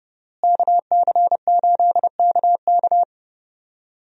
Morse code, heard as a keyed tone, about 30 wpm, 700 Hz. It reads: KC8KK
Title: USA Call Signs - 5 Characters - Repeated 2x - 30wpm